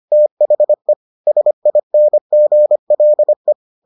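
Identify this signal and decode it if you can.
Morse code, heard as a keyed tone, about 25 wpm, 600 Hz. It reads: THE SINGLE
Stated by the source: Morse edition of the Foundations of Amateur Radio episodes.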